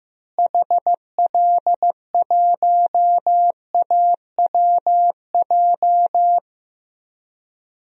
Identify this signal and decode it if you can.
Morse code, heard as a keyed tone, about 15 wpm, 700 Hz. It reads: HL1AWJ
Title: DXCC-100 Call Signs - 6 Characters - Repeated 2x - 15wpm